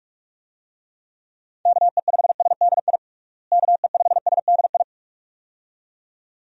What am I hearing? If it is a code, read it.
Morse code, heard as a keyed tone, about 45 wpm, 700 Hz. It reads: KE5SDI KE5SDI
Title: USA Call Signs - 6 Characters - Repeated 2x - 45wpm